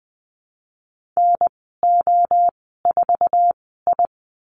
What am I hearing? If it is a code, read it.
Morse code, heard as a keyed tone, about 20 wpm, spaced as for 15 wpm, 700 Hz. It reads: NO4I
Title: 055 Lesson 10g - Letter r - Call Signs 20wpm